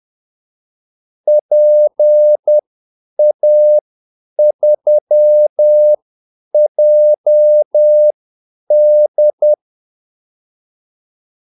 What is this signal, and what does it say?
Morse code, heard as a keyed tone, about 10 wpm, spaced as for 8 wpm, 600 Hz. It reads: PA3JD
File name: Callsigns-10-8wpm